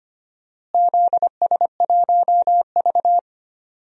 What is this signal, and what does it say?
Morse code, heard as a keyed tone, about 25 wpm, 700 Hz. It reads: ZS1V
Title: DXCC-100 Call Signs - 4 Characters - Repeated 2x - 25wpm